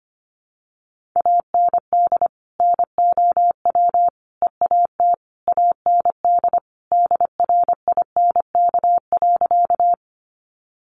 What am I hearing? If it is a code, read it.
Morse code, heard as a keyed tone, about 25 wpm, 700 Hz. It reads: AND NOW EAT AND DRINK.
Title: Sentences from Top 1500 Words 25wpm